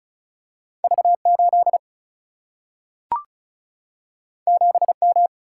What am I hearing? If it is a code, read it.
Morse code, heard as a keyed tone, about 36 wpm, 700 Hz. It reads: V8 7M
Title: Single-Letter Single-Number 35wpm